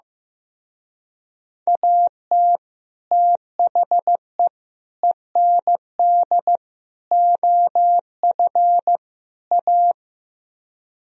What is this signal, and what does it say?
Morse code, heard as a keyed tone, about 16 wpm, 700 Hz. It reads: AT THE END OF A
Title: Speed-Racing - Sets of 5 Words 15wpm